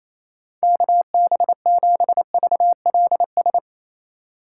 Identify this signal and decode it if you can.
Morse code, heard as a keyed tone, about 28 wpm, 700 Hz. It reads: KB7VLS
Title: USA Call Signs - 6 Characters - 3x Character Spacing - 28wpm